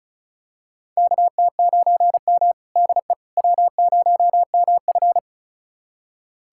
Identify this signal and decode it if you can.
Morse code, heard as a keyed tone, about 36 wpm, 700 Hz. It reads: KT9M DE W0MF
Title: USA Callsign DE Callsign - 4 Characters - 35wpm